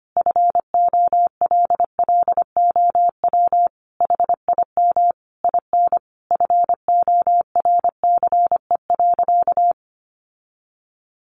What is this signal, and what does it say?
Morse code, heard as a keyed tone, about 25 wpm, 700 Hz. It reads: FOLLOW HIM IN FORCE.
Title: Sentences from Top 700 Words 25wpm